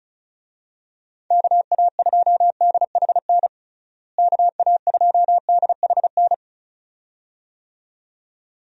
Morse code, heard as a keyed tone, about 34 wpm, 700 Hz. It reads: KA2DHN KA2DHN
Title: USA Call Signs - 6 Characters - Repeated 2x - 35wpm